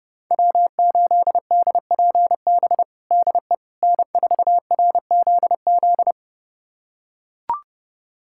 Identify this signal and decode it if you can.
Morse code, heard as a keyed tone, about 32 wpm, 700 Hz. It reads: W8DPB DE N4RZZ
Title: USA Callsign DE Callsign - 5 Characters - 30wpm